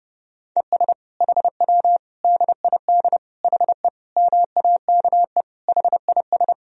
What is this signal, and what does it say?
Morse code, heard as a keyed tone, about 32 wpm, 700 Hz. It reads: ES HW DID HE MAKE HIS
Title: Sentences from Top 400 Words and Abbreviations 30wpm